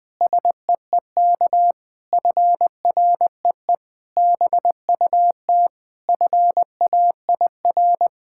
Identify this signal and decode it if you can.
Morse code, heard as a keyed tone, about 20 wpm, 700 Hz. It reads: SEEK FREE BUT FAIR
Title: Sentences from Top 1600 Words 20wpm